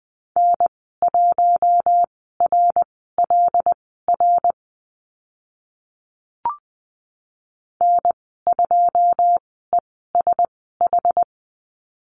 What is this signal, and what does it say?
Morse code, heard as a keyed tone, about 20 wpm, spaced as for 14 wpm, 700 Hz. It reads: N1RLR N2ESH
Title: 082 Lesson 14g - Number 2 - Call Signs 20wpm